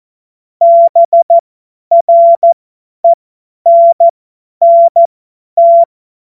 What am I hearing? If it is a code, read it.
Morse code, heard as a keyed tone, about 14 wpm, spaced as for 10 wpm, 680 Hz. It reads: BRENNT